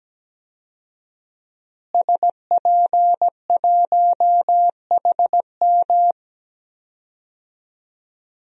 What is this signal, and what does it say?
Morse code, heard as a keyed tone, about 17 wpm, 700 Hz. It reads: SP1HM
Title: DXCC-100 Call Signs - 5 Characters - Repeated 2x - 17wpm